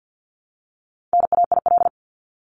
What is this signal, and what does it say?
Morse code, heard as a keyed tone, about 50 wpm, 700 Hz. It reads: NAIL
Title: Top 1501-1750 Words 50wpm